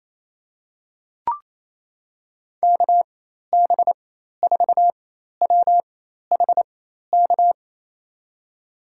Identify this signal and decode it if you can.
Morse code, heard as a keyed tone, about 28 wpm, spaced as for 13 wpm, 700 Hz. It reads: KB4WHK
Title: USA Call Signs - 6 Characters - 4x Character Spacing - 28wpm